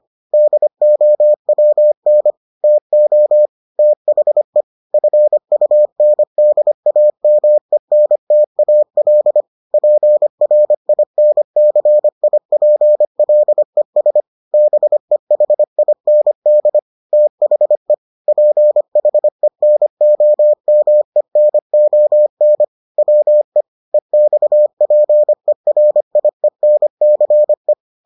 Morse code, heard as a keyed tone, about 25 wpm, 600 Hz. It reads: DOWN TO THE FUNDAMENTAL PRINCIPLES BEHIND THE PHENOMENON WE EXPERIENCE